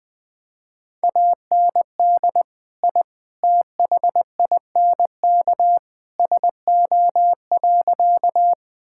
Morse code, heard as a keyed tone, about 20 wpm, 700 Hz. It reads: AND I THINK SO.
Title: Sentences with 3-Word Phrases - 20wpm